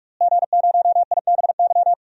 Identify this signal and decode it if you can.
Morse code, heard as a keyed tone, about 47 wpm, 700 Hz. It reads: G0IBY
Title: DXCC-100 Call Signs - 5 Characters - Repeated 2x - 45wpm